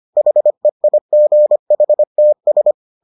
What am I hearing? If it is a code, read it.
Morse code, heard as a keyed tone, about 25 wpm, 600 Hz. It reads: HEIGHTS